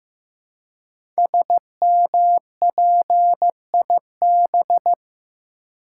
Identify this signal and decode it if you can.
Morse code, heard as a keyed tone, about 15 wpm, 700 Hz. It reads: SMPIB
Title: Code Groups - 5 Letters - 15wpm